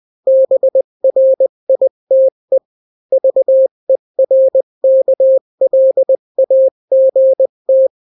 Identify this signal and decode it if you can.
Morse code, heard as a keyed tone, about 20 wpm, 530 Hz. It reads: BRITE VERKLAGT
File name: daily / 18wpm